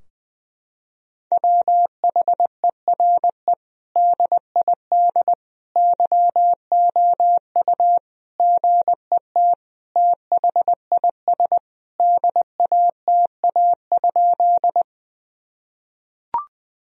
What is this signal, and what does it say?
Morse code, heard as a keyed tone, about 21 wpm, 700 Hz. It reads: WHERE DID YOU GET THIS DATA?
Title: Sentences from Top 1000 Words - Review Part 3 - 20wpm